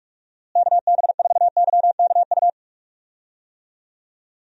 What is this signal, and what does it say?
Morse code, heard as a keyed tone, about 41 wpm, 700 Hz. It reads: KB4YKU
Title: USA Call Signs - 6 Characters - Repeated 2x - 45wpm